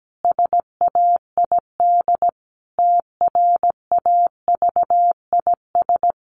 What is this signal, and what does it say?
Morse code, heard as a keyed tone, about 17 wpm, 700 Hz. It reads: SAID TRAVIS
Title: A Sound Of Thunder 17wpm